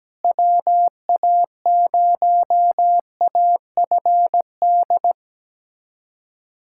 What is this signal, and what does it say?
Morse code, heard as a keyed tone, about 17 wpm, 700 Hz. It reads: WA0AFD
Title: USA Callsign DE Callsign 17wpm